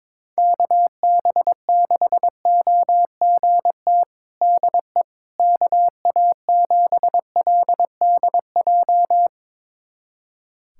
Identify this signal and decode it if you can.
Morse code, heard as a keyed tone, about 22 wpm, 700 Hz. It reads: KB6OGT DE KA7LDJ
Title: USA Callsign DE Callsign - 6 Characters - 22wpm